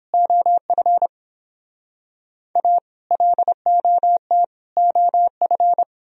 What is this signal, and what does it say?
Morse code, heard as a keyed tone, about 30 wpm, 700 Hz. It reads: OF A LOT OF